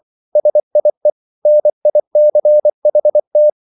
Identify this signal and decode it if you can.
Morse code, heard as a keyed tone, about 24 wpm, 600 Hz. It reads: SIE NICHT